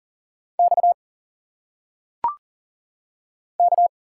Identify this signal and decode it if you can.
Morse code, heard as a keyed tone, about 40 wpm, 700 Hz. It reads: X K
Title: Single Letters - Phonetic Pronunciation 40wpm